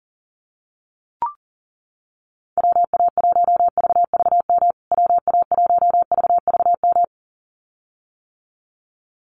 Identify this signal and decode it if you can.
Morse code, heard as a keyed tone, about 37 wpm, 700 Hz. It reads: WA1VVM WA1VVM
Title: USA Call Signs Repeated 2x 40wpm